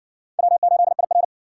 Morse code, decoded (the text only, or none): W8IR